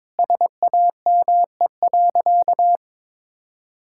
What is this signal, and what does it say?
Morse code, heard as a keyed tone, about 22 wpm, 700 Hz. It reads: SAME.
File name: practice / 22wpm / Sentences with 2-Word Phrases - 22wpm